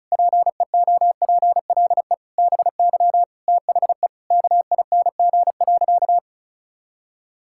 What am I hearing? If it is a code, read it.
Morse code, heard as a keyed tone, about 36 wpm, 700 Hz. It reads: PEOPLE BY THE KING.
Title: Sentences from Top 1000 Words - Review Part 5 - 35wpm